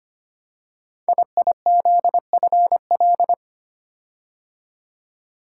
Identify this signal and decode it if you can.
Morse code, heard as a keyed tone, about 25 wpm, 700 Hz. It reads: IIZFL
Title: Code Groups - 5 Letters - 25wpm